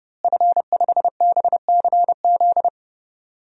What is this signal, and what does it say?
Morse code, heard as a keyed tone, about 30 wpm, 700 Hz. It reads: F5BCZ
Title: DXCC-100 Call Signs - 5 Characters - Repeated 3x - 30wpm